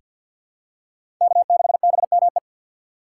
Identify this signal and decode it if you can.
Morse code, heard as a keyed tone, about 51 wpm, 700 Hz. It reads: K6BME
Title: USA Call Signs - 5 Characters - 6x Character Spacing - 50wpm